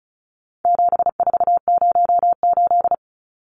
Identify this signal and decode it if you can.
Morse code, heard as a keyed tone, about 34 wpm, 700 Hz. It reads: 7408
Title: Numbers - 4-Digit 35wpm